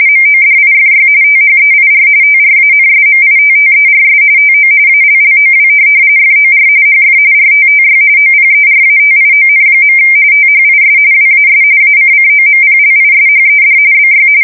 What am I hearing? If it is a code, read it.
RTTY, about 45.45 baud. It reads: MODIFY THEIR TRANSMITTERS TO ALLOW FOR HF RTTY OPERATION. THIS WAS ACCOMPLISHED BY
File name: fsk170 / teletype-RTTY